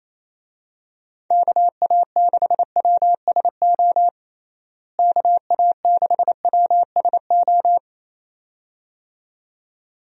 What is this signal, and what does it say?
Morse code, heard as a keyed tone, about 28 wpm, 700 Hz. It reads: KA6WSO KA6WSO
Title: USA Call Signs - 6 Characters - Repeated 2x - 28wpm